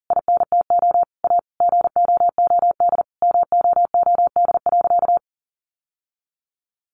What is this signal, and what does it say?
Morse code, heard as a keyed tone, about 40 wpm, 700 Hz. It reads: INTO A GOOD MOOD.
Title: Sentences from Top 1100 Words 40wpm